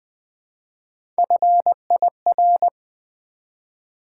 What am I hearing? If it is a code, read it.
Morse code, heard as a keyed tone, about 20 wpm, 700 Hz. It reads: FIR
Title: Build A Word - 1 Letter At A Time - 3 to 5-Letter Words - 20wpm